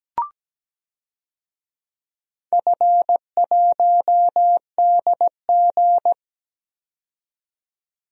Morse code, heard as a keyed tone, about 17 wpm, 700 Hz. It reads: F1DG
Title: DXCC-100 Call Signs - 4 Characters - 17wpm